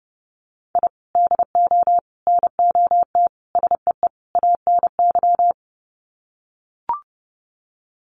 Morse code, heard as a keyed tone, about 30 wpm, 700 Hz. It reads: I DO NOT SEE ANY
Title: Sets of 5 Words 30wpm